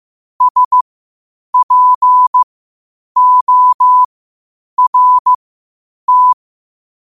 Morse code, heard as a keyed tone, about 15 wpm, spaced as for 8 wpm, 1000 Hz. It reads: SPORT
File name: qotdfiles / 8wpm